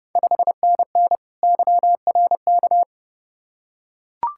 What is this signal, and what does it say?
Morse code, heard as a keyed tone, about 30 wpm, 690 Hz. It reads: 5NN YRK
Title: Ontario QSO Party Exchange 30wpm